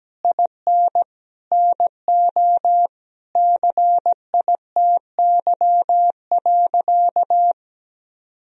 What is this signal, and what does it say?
Morse code, heard as a keyed tone, about 17 wpm, 700 Hz. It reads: IN NO CITY.
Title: Sentences from Top 1000 Words - Review Part 1 - 17wpm